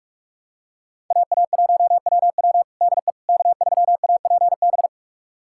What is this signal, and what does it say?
Morse code, heard as a keyed tone, about 42 wpm, 700 Hz. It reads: AA1WW DE K3APB
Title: USA Callsign DE Callsign - 5 Characters - 45wpm